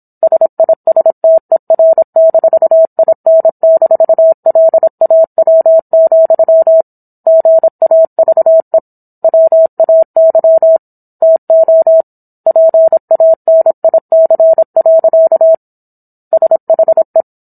Morse code, heard as a keyed tone, about 27 wpm, 650 Hz. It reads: SISTER-IN-LAW, GAVE WAY TO PANIC. SHE